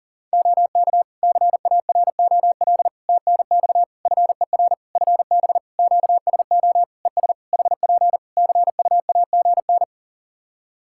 Morse code, heard as a keyed tone, about 40 wpm, 700 Hz. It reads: OK CAROL TNX FER FB QSO ES HP CUAGN
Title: Practice QSO - Exchange 4 - Ending - 40wpm